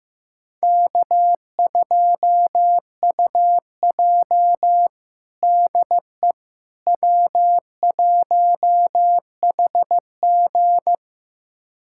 Morse code, heard as a keyed tone, about 16 wpm, 700 Hz. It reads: K2UJ DE W1HG
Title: USA Callsign DE Callsign - 4 Characters - 15wpm